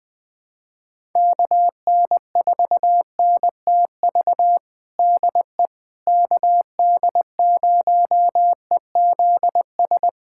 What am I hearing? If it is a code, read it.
Morse code, heard as a keyed tone, about 20 wpm, 700 Hz. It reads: KN4NTV DE KD0EZS
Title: USA Callsign DE Callsign 20wpm